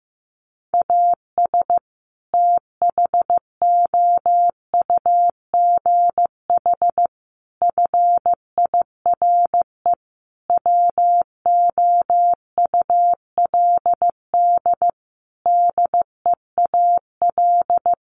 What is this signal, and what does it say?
Morse code, heard as a keyed tone, about 15 wpm, 700 Hz. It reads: AS THOUGH FIRE WOULD DEAL
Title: Sentences from Top 1500 Words 15wpm